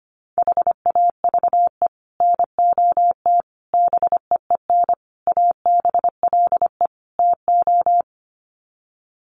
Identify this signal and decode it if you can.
Morse code, heard as a keyed tone, about 25 wpm, 700 Hz. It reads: HAVE NOT BEEN ABLE TO
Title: Sets of 5 Words 25wpm